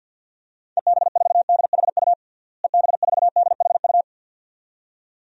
Morse code, heard as a keyed tone, about 52 wpm, 700 Hz. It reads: EB4DHU EB4DHU
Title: DXCC-100 Call Signs - 6 Characters - Repeated 2x - 50wpm